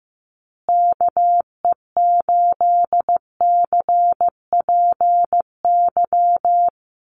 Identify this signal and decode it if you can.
Morse code, heard as a keyed tone, about 15 wpm, 700 Hz. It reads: KE8CPY
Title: USA Call Signs Repeated 3x 15wpm